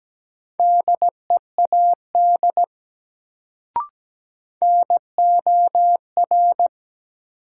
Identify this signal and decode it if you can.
Morse code, heard as a keyed tone, about 17 wpm, 700 Hz. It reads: DEAD NOR
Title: Sentences from Top 1500 Words - New Word Drill Alt - 17wpm